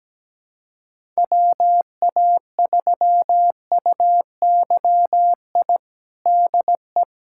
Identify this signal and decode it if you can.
Morse code, heard as a keyed tone, about 17 wpm, 700 Hz. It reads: WA3UYI DE
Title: USA Callsign DE Callsign - 6 Characters - 17wpm